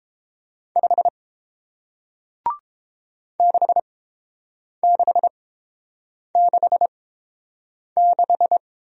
Morse code, heard as a keyed tone, about 31 wpm, spaced as for 7 wpm, 700 Hz. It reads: 5 6666